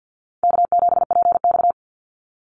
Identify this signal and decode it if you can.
Morse code, heard as a keyed tone, about 48 wpm, 700 Hz. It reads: K7PX
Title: NA SOTA Chaser Callsigns 50wpm